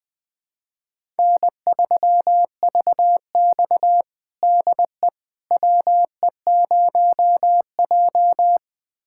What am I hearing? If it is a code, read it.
Morse code, heard as a keyed tone, about 20 wpm, 700 Hz. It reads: N3VX DE WE0J
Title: USA Callsign DE Callsign - 4 Characters - 20wpm